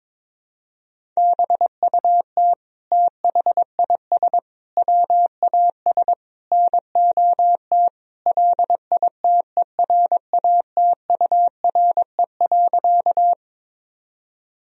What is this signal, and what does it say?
Morse code, heard as a keyed tone, about 22 wpm, 700 Hz. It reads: BUT THIS WAS NOT LITERATURE.
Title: Sentences from Top 1000 Words - Review Part 3 - 22wpm